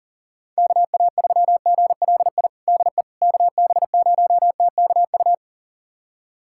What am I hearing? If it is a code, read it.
Morse code, heard as a keyed tone, about 37 wpm, 700 Hz. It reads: KA3GLI DE KB0TKU
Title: USA Callsign DE Callsign - 6 Characters - 40wpm